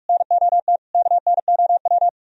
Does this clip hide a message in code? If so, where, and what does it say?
Yes: Morse code from 0.1 to 2.1 s, heard as a keyed tone, about 58 wpm, 700 Hz. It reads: NOT KNOW